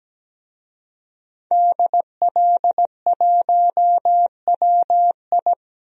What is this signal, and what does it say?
Morse code, heard as a keyed tone, about 17 wpm, 700 Hz. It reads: DL1WI
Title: DXCC-100 Call Signs - 5 Characters - 17wpm